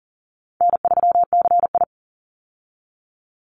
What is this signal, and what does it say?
Morse code, heard as a keyed tone, about 40 wpm, 700 Hz. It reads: N3CI
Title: Easy Call Signs Repeated 3x - 40wpm